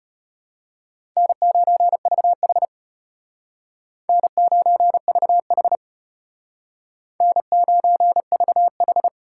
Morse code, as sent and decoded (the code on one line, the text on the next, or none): -. ----. ...- .... / -. ----. ...- .... / -. ----. ...- ....
N9VH N9VH N9VH